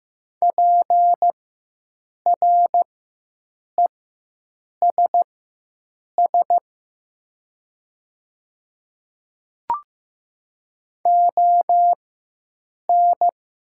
Morse code, heard as a keyed tone, about 15 wpm, spaced as for 7 wpm, 700 Hz. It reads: PRESS ON
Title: 4x Character Spacing - Top 300 Words - 15wpm